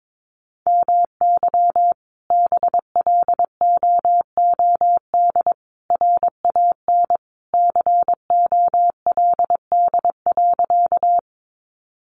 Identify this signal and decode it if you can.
Morse code, heard as a keyed tone, about 22 wpm, 700 Hz. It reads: MY BLOOD RAN COLD.
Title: Sentences from Top 1100 Words 22wpm